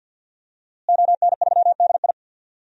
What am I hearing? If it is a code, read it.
Morse code, heard as a keyed tone, about 52 wpm, 700 Hz. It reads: ON3DI
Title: DXCC-100 Call Signs - 5 Characters - Repeated 3x - 50wpm